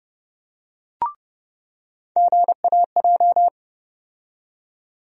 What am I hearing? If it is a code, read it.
Morse code, heard as a keyed tone, about 30 wpm, 700 Hz. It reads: GAJ
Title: USA Call Sign Postfixes - 3 Letters 30wpm